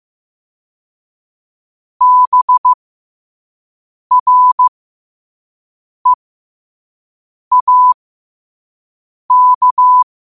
Morse code, heard as a keyed tone, about 15 wpm, spaced as for 5 wpm, 1000 Hz. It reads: BREAK